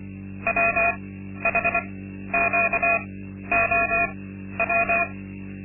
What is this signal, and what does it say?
Morse code, heard as a keyed tone, about 24 wpm, spaced as for 12 wpm, 680 Hz, with noise. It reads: WHQOW
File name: Linksmakalnis-2